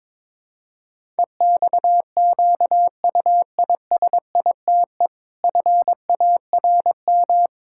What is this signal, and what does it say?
Morse code, heard as a keyed tone, about 22 wpm, 700 Hz. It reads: EXQUISITE FARM
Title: Pangram Challenge v2 - 22wpm